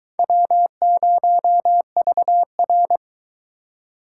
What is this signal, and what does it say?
Morse code, heard as a keyed tone, about 22 wpm, 700 Hz. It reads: W0VR